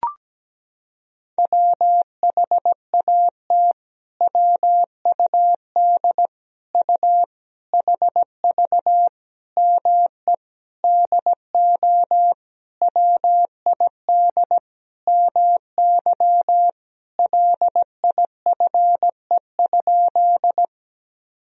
Morse code, heard as a keyed tone, about 17 wpm, 700 Hz. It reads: WHAT WUD U HV ME DO WID MY LIFE?
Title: Sentences from Top 300 Words and Abbreviations 17wpm